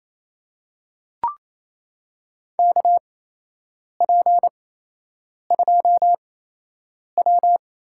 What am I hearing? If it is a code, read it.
Morse code, heard as a keyed tone, about 28 wpm, spaced as for 8 wpm, 700 Hz. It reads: KP2W
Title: USA Call Signs - 4 Characters - 8x Character Spacing - 28wpm